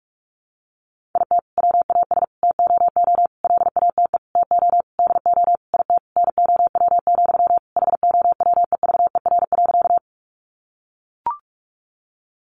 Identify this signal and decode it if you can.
Morse code, heard as a keyed tone, about 44 wpm, 700 Hz. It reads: IT WAS TOO LATE TO DO IT NOW, HOWEVER.
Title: Sentences from Top 600 Words 45wpm